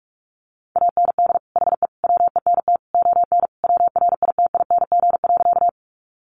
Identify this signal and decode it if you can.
Morse code, heard as a keyed tone, about 44 wpm, 700 Hz. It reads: AND HE WENT ON WRITING.
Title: Sentences from Top 800 Words 45wpm